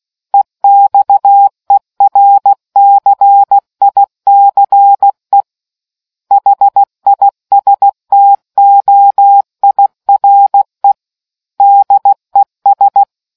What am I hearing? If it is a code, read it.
Morse code, heard as a keyed tone, about 16 wpm, 800 Hz. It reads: EXERCICE HISTOIRE DES